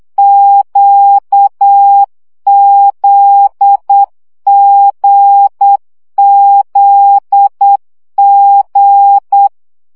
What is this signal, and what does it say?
Morse code, heard as a keyed tone, about 8 wpm, 800 Hz. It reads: QZGZG
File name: CW gyakorlas g z q 7 baud